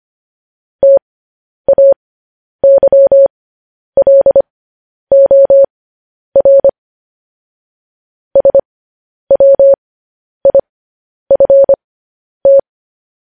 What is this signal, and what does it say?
Morse code, heard as a keyed tone, about 25 wpm, spaced as for 10 wpm, 550 Hz. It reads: TAYLOR SWIFT